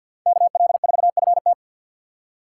Morse code, heard as a keyed tone, about 64 wpm, 700 Hz. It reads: KL4FT